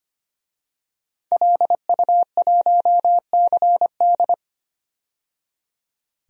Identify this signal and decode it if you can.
Morse code, heard as a keyed tone, about 25 wpm, 700 Hz. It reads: LU1CD
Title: DXCC-100 Call Signs - 5 Characters - 25wpm